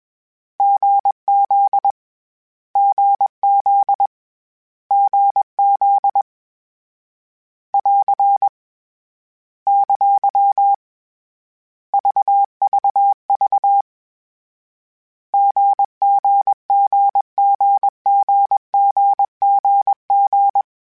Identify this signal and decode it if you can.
Morse code, heard as a keyed tone, about 21 wpm, 800 Hz. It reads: GZ GZ GZ <AR> ! VVV GGGGGGGG
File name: Short Runs / Short Runs-mp3s / GZ GZ7 @20